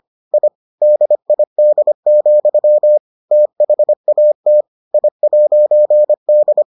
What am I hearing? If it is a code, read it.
Morse code, heard as a keyed tone, about 25 wpm, 600 Hz. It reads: I DID, THAT I'D